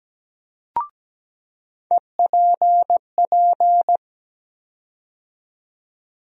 Morse code, heard as a keyed tone, about 17 wpm, 700 Hz. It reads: EPP
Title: USA Call Sign Postfixes - 3 Letters 17wpm